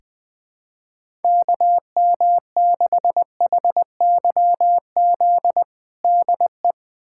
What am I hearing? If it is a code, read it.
Morse code, heard as a keyed tone, about 21 wpm, 700 Hz. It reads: KM6HYZ DE
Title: USA Callsign DE Callsign - 6 Characters - 20wpm